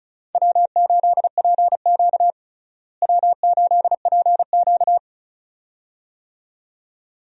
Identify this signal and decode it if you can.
Morse code, heard as a keyed tone, about 34 wpm, 700 Hz. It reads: W8PQ W8PQ
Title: USA Call Signs - 4 Characters - Repeated 2x - 35wpm